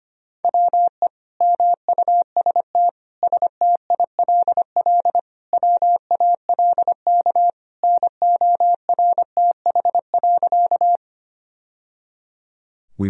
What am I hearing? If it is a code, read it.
Morse code, heard as a keyed tone, about 25 wpm, 700 Hz. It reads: WE MUST STILL WALK NORTH.
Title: Sentences from Top 1000 Words - Review Part 1 - 25wpm